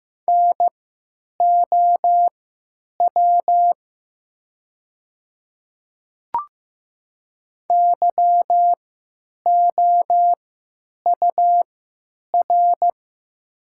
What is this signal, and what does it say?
Morse code, heard as a keyed tone, about 15 wpm, spaced as for 8 wpm, 700 Hz. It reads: NOW YOUR
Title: 3x Character Spacing - Top 100 Words - 15wpm